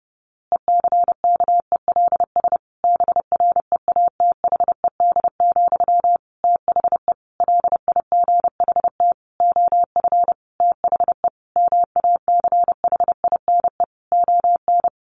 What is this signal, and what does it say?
Morse code, heard as a keyed tone, about 30 wpm, 700 Hz. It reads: ECKELS BREATHED, THE LIGHT OF THE MACHINE ON
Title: A Sound Of Thunder 30wpm